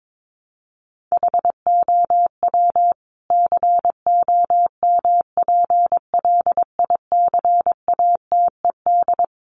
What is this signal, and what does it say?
Morse code, heard as a keyed tone, about 22 wpm, 700 Hz. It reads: HOW COMPLICATED
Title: Sentences from Top 1200 Words 22wpm